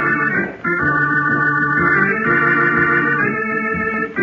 SSTV (slow-scan television), Robot 36.